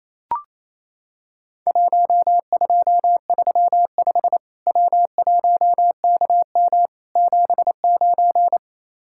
Morse code, heard as a keyed tone, about 27 wpm, 700 Hz. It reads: 1235 W1KM 79
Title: Worked All Europe DX Contest - QTC Exchange - 28wpm